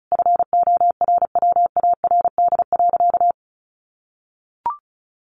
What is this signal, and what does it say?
Morse code, heard as a keyed tone, about 34 wpm, 700 Hz. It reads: FORWARD.
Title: Sentences from Top 1300 Words 35wpm